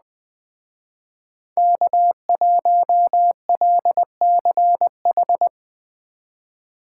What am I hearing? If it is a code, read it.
Morse code, heard as a keyed tone, about 21 wpm, 700 Hz. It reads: K1LCH
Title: Speed Racing - USA 1x3 Call Signs 17wpm